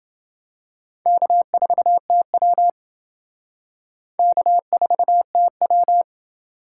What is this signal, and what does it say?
Morse code, heard as a keyed tone, about 28 wpm, 700 Hz. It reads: K4TW K4TW